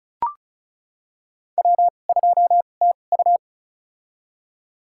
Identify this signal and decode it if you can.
Morse code, heard as a keyed tone, about 34 wpm, spaced as for 25 wpm, 700 Hz. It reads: W2TU
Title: USA Call Signs - 4 Characters - 2x Character Spacing - 35wpm